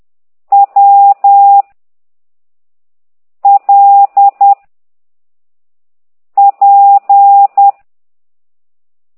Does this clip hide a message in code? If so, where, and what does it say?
Yes: Morse code from 0.5 to 7.7 s, heard as a keyed tone, about 10 wpm, 800 Hz. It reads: W L P